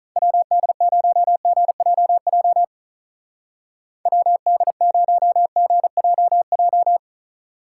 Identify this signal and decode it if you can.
Morse code, heard as a keyed tone, about 41 wpm, 700 Hz. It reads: WD0GJJ WD0GJJ
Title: Speed Racing - USA 2x3 Call Signs 35wpm